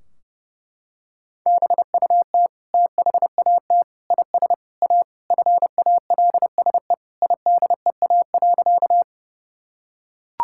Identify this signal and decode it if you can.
Morse code, heard as a keyed tone, about 30 wpm, 700 Hz. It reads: BUT THAT IS A FALSE IDEA.
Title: Sentences from Top 1600 Words 30wpm